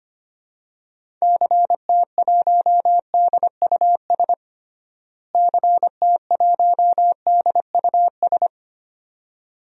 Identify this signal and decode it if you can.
Morse code, heard as a keyed tone, about 25 wpm, 700 Hz. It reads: CT1DUS CT1DUS
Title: DXCC-100 Call Signs - 6 Characters - Repeated 2x - 25wpm